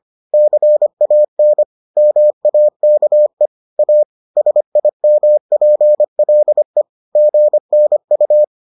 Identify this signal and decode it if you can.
Morse code, heard as a keyed tone, about 25 wpm, 600 Hz. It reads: CAN MAKE A SIMPLE GNU